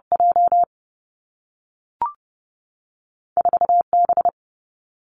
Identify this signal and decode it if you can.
Morse code, heard as a keyed tone, about 30 wpm, 700 Hz. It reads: J 4 B
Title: Two Challenging Letter-Number Combinations 30wpm